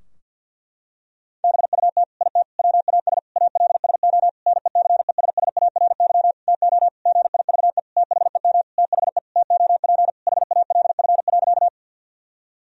Morse code, heard as a keyed tone, about 50 wpm, 700 Hz. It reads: BUT IT WAS ALSO NECESSARY TO GIVE THEM THE TOP HALF.
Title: Sentences from Top 1000 Words - Review Part 2 - 50wpm